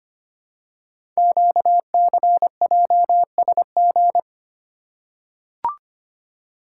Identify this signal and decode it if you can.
Morse code, heard as a keyed tone, about 25 wpm, 700 Hz. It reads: QCJSG
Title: Code Groups - 5 Letters - 25wpm